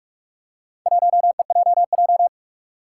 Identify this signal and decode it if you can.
Morse code, heard as a keyed tone, about 44 wpm, 700 Hz. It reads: 1EJJ
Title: Four Challenging Letter-Number Combinations 45wpm